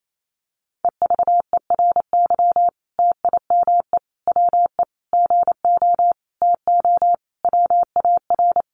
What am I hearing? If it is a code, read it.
Morse code, heard as a keyed tone, about 28 wpm, 700 Hz. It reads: EVERY TIME WE GO TO WAR
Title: Sentences from Top 1300 Words 28wpm